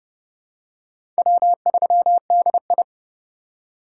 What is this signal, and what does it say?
Morse code, heard as a keyed tone, about 31 wpm, 700 Hz. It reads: W3DI
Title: USA Call Signs - 2x Character Spacing - 30wpm